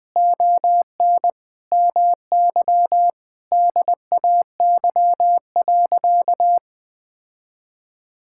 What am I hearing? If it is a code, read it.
Morse code, heard as a keyed tone, about 20 wpm, 700 Hz. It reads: ON MY DAY.
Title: Sentences with 9-Word Phrases - Spoken Beginning - 20wpm